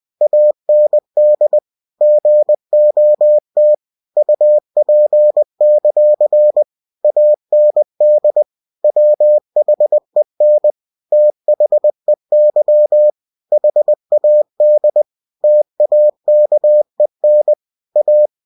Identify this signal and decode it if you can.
Morse code, heard as a keyed tone, about 20 wpm, 600 Hz. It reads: AND GOT UP; AND WHEN THEY HAD TAKEN A
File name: cw-books / winnie-the-pooh / winnie-20wpm-page-0036